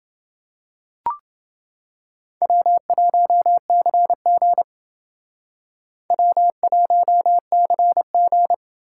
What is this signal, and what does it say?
Morse code, heard as a keyed tone, about 29 wpm, 700 Hz. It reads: W1CG W1CG